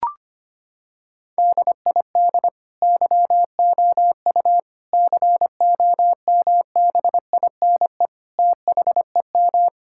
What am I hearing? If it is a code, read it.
Morse code, heard as a keyed tone, about 25 wpm, 700 Hz. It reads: DID YOU COMBINE THEM
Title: Sentences from Top 1600 Words 25wpm